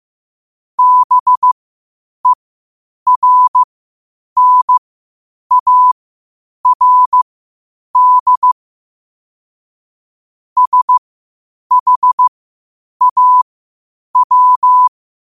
Morse code, heard as a keyed tone, about 15 wpm, spaced as for 8 wpm, 1000 Hz. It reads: BERNARD SHAW
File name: 8wpm